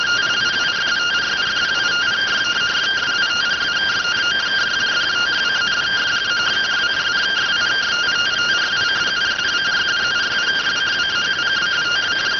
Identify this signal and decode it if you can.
Two-tone teleprinter signal, about 170 Hz shift.